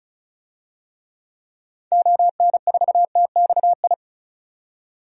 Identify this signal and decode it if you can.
Morse code, heard as a keyed tone, about 36 wpm, 700 Hz. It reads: ON4TXI
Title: DXCC-100 Call Signs - 6 Characters - 35wpm